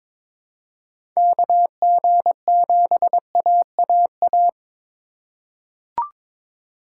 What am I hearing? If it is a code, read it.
Morse code, heard as a keyed tone, about 22 wpm, 700 Hz. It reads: KG7AAA
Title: USA Call Signs - 6 Characters - 2x Character Spacing - 22wpm